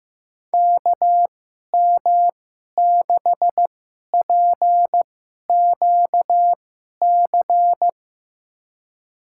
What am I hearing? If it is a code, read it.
Morse code, heard as a keyed tone, about 16 wpm, spaced as for 11 wpm, 700 Hz. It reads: KM6PQC
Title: USA Call Signs - 6 Characters - 2x Character Spacing - 15wpm